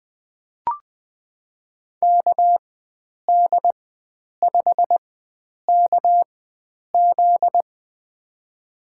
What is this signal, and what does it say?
Morse code, heard as a keyed tone, about 20 wpm, spaced as for 9 wpm, 700 Hz. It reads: KD5KZ
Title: USA Call Signs - 5 Characters - 4x Character Spacing - 20wpm